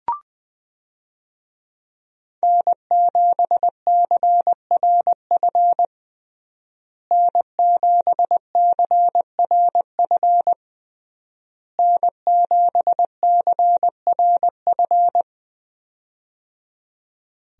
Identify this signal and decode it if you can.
Morse code, heard as a keyed tone, about 20 wpm, 700 Hz. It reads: N7CRF N7CRF N7CRF
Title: USA Call Signs - 5 Characters - Repeated 3x - 20wpm